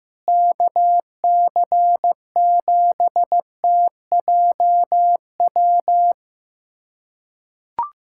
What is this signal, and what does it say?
Morse code, heard as a keyed tone, about 16 wpm, 700 Hz. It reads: KC7TJW
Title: USA Callsign DE Callsign - 6 Characters - 15wpm